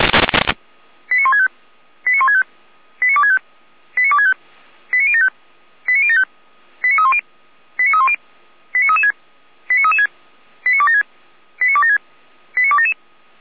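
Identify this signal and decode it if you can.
Five-tone selective call (ZVEI), heard as quick runs of five tones, tones of about 70 ms. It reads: 89165 89165 89265 89265 89075 89075 89310 89310 89336 89336 89276 89276 89277
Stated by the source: Received by SDR, ZVEI five-tone selective call, feuerwehr-orlamuende fire brigade (Orlamünde) audio archive.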